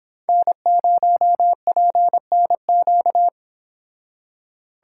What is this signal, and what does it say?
Morse code, heard as a keyed tone, about 27 wpm, 700 Hz. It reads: N0PNQ
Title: Speed Racing - USA 1x3 Call Signs 22wpm